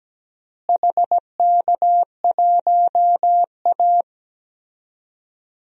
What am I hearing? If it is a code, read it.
Morse code, heard as a keyed tone, about 17 wpm, 700 Hz. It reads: HK1A
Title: DXCC-100 Call Signs - 4 Characters - Repeated 2x - 17wpm